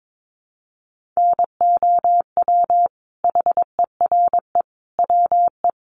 Morse code, heard as a keyed tone, about 22 wpm, 700 Hz. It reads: NOW HERE WE
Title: Sentences from Top 1200 Words 22wpm